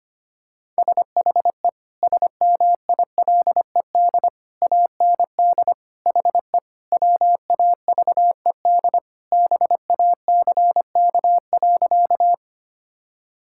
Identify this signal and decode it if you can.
Morse code, heard as a keyed tone, about 25 wpm, 690 Hz. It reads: SHE SMILED AND HE WAVED BACK.
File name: practice / 25wpm / Sentences with 2-Word Phrases - 25wpm